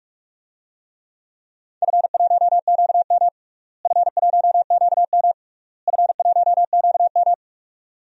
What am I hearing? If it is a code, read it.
Morse code, heard as a keyed tone, about 46 wpm, 700 Hz. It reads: F1QM F1QM F1QM
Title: DXCC-100 Call Signs - 4 Characters - Repeated 3x - 45wpm